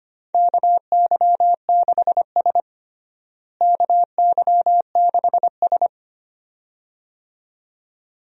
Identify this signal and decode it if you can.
Morse code, heard as a keyed tone, about 25 wpm, 700 Hz. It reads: KY6S KY6S
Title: USA Call Signs - 4 Characters - Repeated 2x - 25wpm